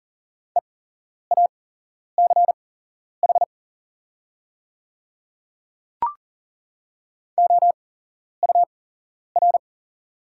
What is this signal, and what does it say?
Morse code, heard as a keyed tone, about 40 wpm, spaced as for 11 wpm, 690 Hz. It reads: EACH OUR
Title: 8x Character Spacing - Top 300 Words - 40wpm